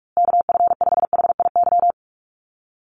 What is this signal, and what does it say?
Morse code, heard as a keyed tone, about 44 wpm, 700 Hz. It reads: KF5HIY
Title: USA Call Signs Repeated 3x 45wpm